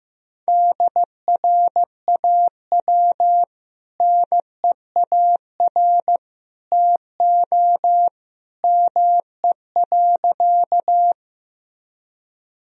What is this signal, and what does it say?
Morse code, heard as a keyed tone, about 16 wpm, 700 Hz. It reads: DRAW NEAR TO ME.
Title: Sentences from Top 1000 Words - Review Part 1 - 15wpm